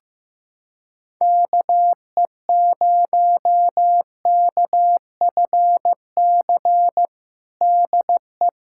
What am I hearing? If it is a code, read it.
Morse code, heard as a keyed tone, about 16 wpm, 700 Hz. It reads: KE0KFC DE
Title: USA Callsign DE Callsign - 6 Characters - 15wpm